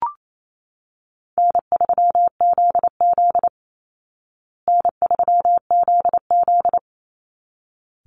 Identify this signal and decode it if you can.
Morse code, heard as a keyed tone, about 28 wpm, 700 Hz. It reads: N3ZZ N3ZZ
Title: Easy Call Signs Repeated 2x - 28wpm